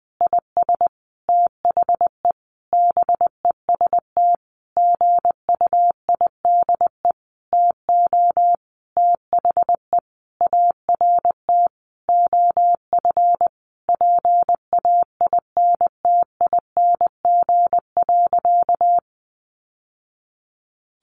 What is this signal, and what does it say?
Morse code, heard as a keyed tone, about 20 wpm, 700 Hz. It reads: IS THE BEST GUIDE TO THE ART OF PAINTING.
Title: Sentences from Top 1200 Words 20wpm